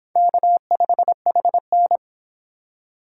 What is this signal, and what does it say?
Morse code, heard as a keyed tone, about 30 wpm, 700 Hz. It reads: K5HN